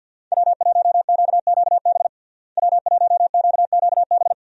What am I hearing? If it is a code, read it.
Morse code, heard as a keyed tone, about 53 wpm, 700 Hz. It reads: W1QQB W1QQB
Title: USA Call Signs - 5 Characters - Repeated 2x - 50wpm